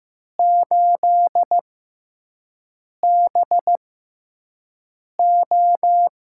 Morse code, heard as a keyed tone, about 15 wpm, spaced as for 5 wpm, 700 Hz. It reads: 8BO
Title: USA Call Signs - 5 Characters - 6x Character Spacing - 15wpm